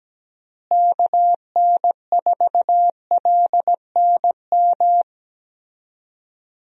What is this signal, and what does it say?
Morse code, heard as a keyed tone, about 17 wpm, 700 Hz. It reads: KN4LNM
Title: USA Call Signs - 6 Characters - Repeated 2x - 17wpm